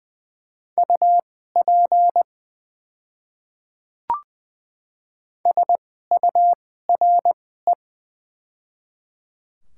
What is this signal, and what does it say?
Morse code, heard as a keyed tone, about 20 wpm, spaced as for 14 wpm, 700 Hz. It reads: UP SURE
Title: 2x Character Spacing - Top 300 Words - 20wpm